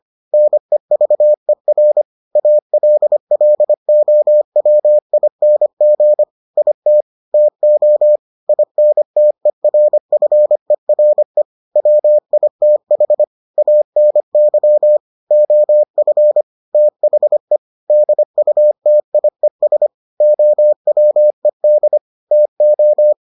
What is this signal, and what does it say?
Morse code, heard as a keyed tone, about 25 wpm, 600 Hz. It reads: NEVER ALLOWING IT TO INTERFERE WITH ANY OF THE DUTIES OWED TO